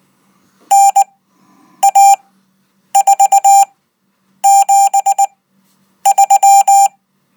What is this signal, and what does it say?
Morse code, heard as a keyed tone, about 19 wpm, spaced as for 8 wpm, 790 Hz. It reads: NA473